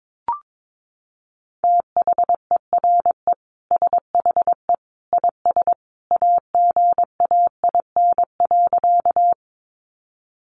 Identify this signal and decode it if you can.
Morse code, heard as a keyed tone, about 22 wpm, 700 Hz. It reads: THERE SHE IS AGAIN.
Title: Sentences from Top 200 Words 22wpm